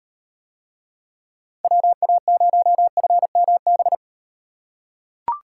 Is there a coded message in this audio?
Yes: Morse code from 1.6 to 4.0 s, heard as a keyed tone, about 36 wpm, 700 Hz. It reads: WA0FMB